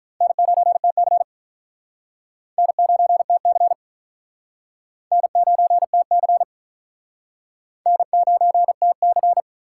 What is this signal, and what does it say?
Morse code, heard as a keyed tone, about 45 wpm, 700 Hz. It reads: N9TC N9TC N9TC N9TC